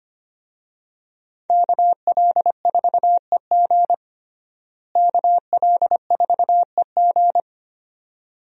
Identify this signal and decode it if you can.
Morse code, heard as a keyed tone, about 25 wpm, 700 Hz. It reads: KL4EG KL4EG
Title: USA Call Signs - 5 Characters - Repeated 2x - 25wpm